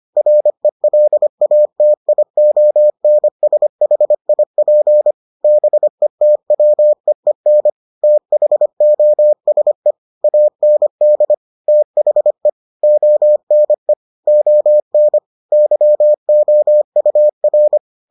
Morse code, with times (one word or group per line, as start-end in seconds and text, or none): 0.2-5.1 RELATIONSHIP
5.4-7.7 BETWEEN
8.0-9.9 THOSE
10.2-11.3 AND
11.7-12.5 THE
12.8-13.9 ONE
14.3-15.2 ON
15.5-17.8 YOUR